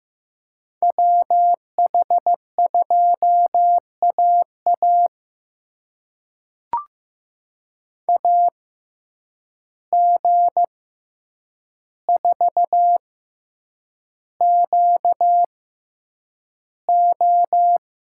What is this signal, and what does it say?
Morse code, heard as a keyed tone, about 15 wpm, spaced as for 8 wpm, 700 Hz. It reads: WH2AA AG4QO
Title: USA Call Signs - 5 Characters - 6x Character Spacing - 15wpm